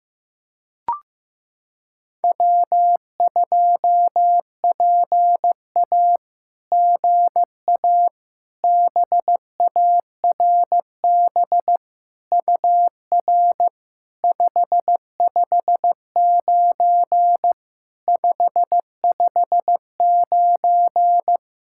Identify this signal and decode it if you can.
Morse code, heard as a keyed tone, about 15 wpm, 700 Hz. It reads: W2PA GA BARB UR 559 559
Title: SOTA QSO Practice 15wpm